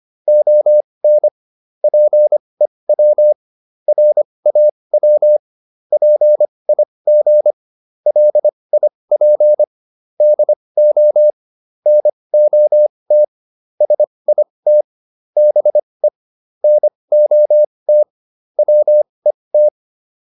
Morse code, heard as a keyed tone, about 25 wpm, spaced as for 20 wpm, 600 Hz. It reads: ON PEW RAW PIG LIP DO NOT SIT BE NOT WET